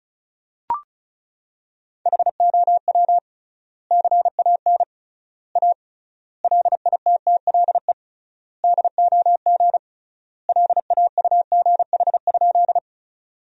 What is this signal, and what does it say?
Morse code, heard as a keyed tone, about 36 wpm, 700 Hz. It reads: HOW CAN A LITTLE DOG LAUGH?
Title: 3x Word-Spacing - Sentences from Top 500 Words - 35wpm